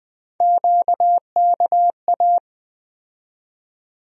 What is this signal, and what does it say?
Morse code, heard as a keyed tone, about 20 wpm, 700 Hz. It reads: QKA
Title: USA Call Sign Postfixes - 3 Letters 20wpm